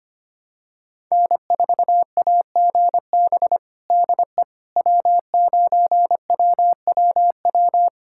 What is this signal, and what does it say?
Morse code, heard as a keyed tone, about 25 wpm, 700 Hz. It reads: N4AGB DE W9WWW
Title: USA Callsign DE Callsign - 5 Characters - 25wpm